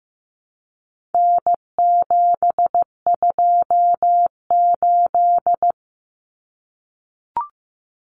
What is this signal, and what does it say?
Morse code, heard as a keyed tone, about 15 wpm, 700 Hz. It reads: N728
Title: Common Cut Numbers - 4-Digit 15wpm